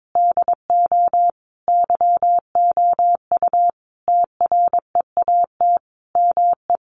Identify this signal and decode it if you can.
Morse code, heard as a keyed tone, about 22 wpm, 700 Hz. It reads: DO YOU TREAT ME
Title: Sentences from Top 1300 Words 22wpm